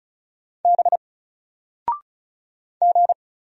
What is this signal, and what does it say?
Morse code, heard as a keyed tone, about 33 wpm, 700 Hz. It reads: B G
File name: Single-Letter-Number-35wpm